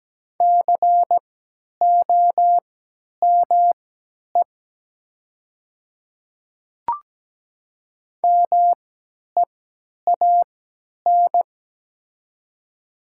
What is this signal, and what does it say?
Morse code, heard as a keyed tone, about 17 wpm, spaced as for 10 wpm, 700 Hz. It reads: COME MEAN
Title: 3x Character Spacing - Top 200 Words - 17wpm